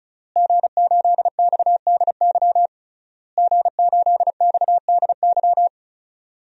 Morse code, heard as a keyed tone, about 34 wpm, 700 Hz. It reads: G8XDY G8XDY
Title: DXCC-100 Call Signs - 5 Characters - Repeated 3x - 35wpm